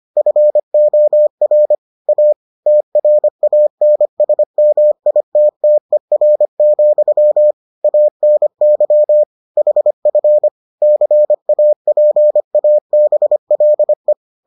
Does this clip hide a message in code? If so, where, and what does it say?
Yes: Morse code from 0.2 to 14.1 s, heard as a keyed tone, about 25 wpm, 600 Hz. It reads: FOR A TRANSMITTER, ANY HF CAPABLE